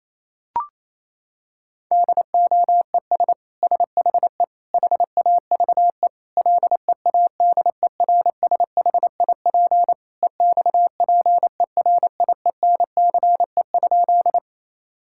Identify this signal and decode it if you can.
Morse code, heard as a keyed tone, about 28 wpm, 700 Hz. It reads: DOES SHE HAVE LEADERSHIP EXPERIENCE?
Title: Sentences from Top 1600 Words 28wpm